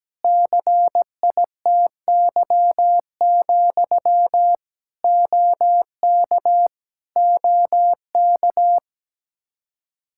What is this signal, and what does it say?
Morse code, heard as a keyed tone, about 17 wpm, 700 Hz. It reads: CITY, OK OK
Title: Practice QSO - Exchange 1 - Beginning - 17wpm